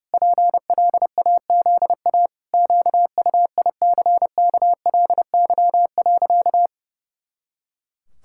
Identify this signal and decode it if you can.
Morse code, heard as a keyed tone, about 30 wpm, 700 Hz. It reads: PLAZA QUICKLY.
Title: Pangram Challenge v2 - 30wpm